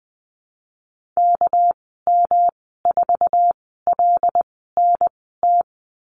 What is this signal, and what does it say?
Morse code, heard as a keyed tone, about 20 wpm, spaced as for 14 wpm, 700 Hz. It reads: KM4LNT
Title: 231 Lesson 37g - Letter x - Call Signs 20wpm